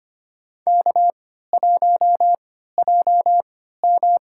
Morse code, heard as a keyed tone, about 25 wpm, spaced as for 14 wpm, 700 Hz. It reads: K1JM
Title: USA Call Signs - 3x Character Spacing - 25wpm